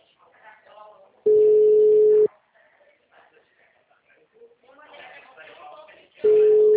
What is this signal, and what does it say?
Ringback tone (Europe, 425 Hz), with speech.
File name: out- -2016- - - .29345